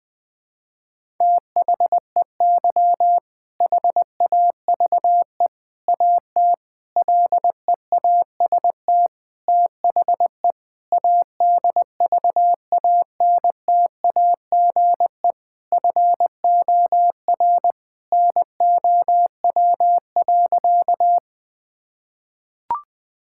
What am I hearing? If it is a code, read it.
Morse code, heard as a keyed tone, about 20 wpm, 700 Hz. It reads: THEY HAVE AT LEAST THE ADVANTAGE FOR NOW.
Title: Sentences from Top 1000 Words - Review Part 6 - 20wpm